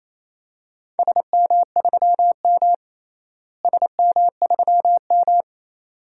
Morse code, heard as a keyed tone, about 28 wpm, 700 Hz. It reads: SM3M SM3M
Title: DXCC-100 Call Signs - 4 Characters - Repeated 2x - 28wpm